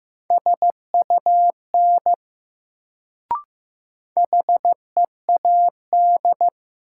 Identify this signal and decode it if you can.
Morse code, heard as a keyed tone, about 15 wpm, 700 Hz. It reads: SUN HEAD
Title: Sentences from Top 300 Words - New Word Drill - 15wpm